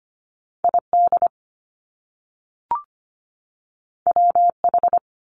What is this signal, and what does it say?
Morse code, heard as a keyed tone, about 25 wpm, 700 Hz. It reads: ID WH
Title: Top Two Letter Combinations in Words 25wpm